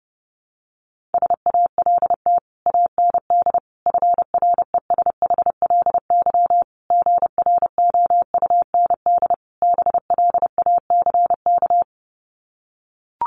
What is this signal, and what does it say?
Morse code, heard as a keyed tone, about 30 wpm, 700 Hz. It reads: SALT AND FRESHLY GROUND BLACK
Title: Sets of 5 Words Encore 30wpm